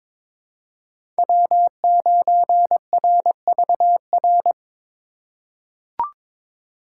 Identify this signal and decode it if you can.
Morse code, heard as a keyed tone, about 22 wpm, 700 Hz. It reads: W9RVR
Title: USA Call Signs - 5 Characters - 2x Character Spacing - 22wpm